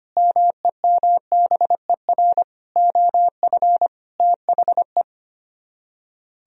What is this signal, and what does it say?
Morse code, heard as a keyed tone, about 33 wpm, 700 Hz. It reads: MEMBER OF THE